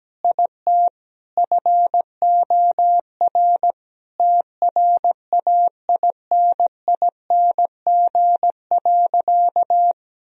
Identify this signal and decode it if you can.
Morse code, heard as a keyed tone, about 17 wpm, 700 Hz. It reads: IT FOR TRAINING.
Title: Sentences from Top 1000 Words - Review Part 4 - 17wpm